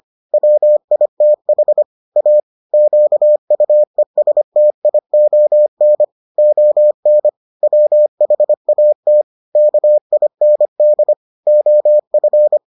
Morse code, heard as a keyed tone, about 25 wpm, 600 Hz. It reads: WITH A QUESTION ON WHAT KIND OF